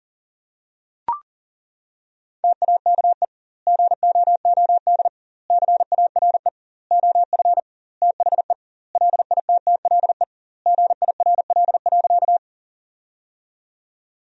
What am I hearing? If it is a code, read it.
Morse code, heard as a keyed tone, about 40 wpm, 700 Hz. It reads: TAKE GOOD CARE OF THE LITTLE GIRL.
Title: 2x Word-Spacing - Sentences from Top 400 Words - 40wpm